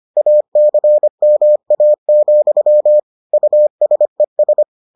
Morse code, heard as a keyed tone, about 25 wpm, 600 Hz. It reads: ACMA, USES